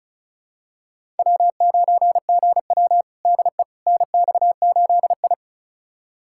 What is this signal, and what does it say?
Morse code, heard as a keyed tone, about 36 wpm, 700 Hz. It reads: W9GW DE NX8I
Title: USA Callsign DE Callsign - 4 Characters - 35wpm